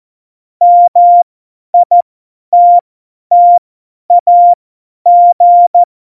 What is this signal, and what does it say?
Morse code, heard as a keyed tone, about 14 wpm, spaced as for 10 wpm, 700 Hz. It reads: MITTAG
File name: Morsekurs / Witze / Witz 17.Juli 2018, 50 F70